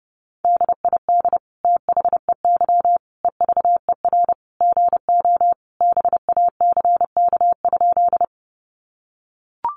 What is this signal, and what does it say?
Morse code, heard as a keyed tone, about 30 wpm, 700 Hz. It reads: DID THEY EVER GO BACK?
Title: Sentences from Top 400 Words 30wpm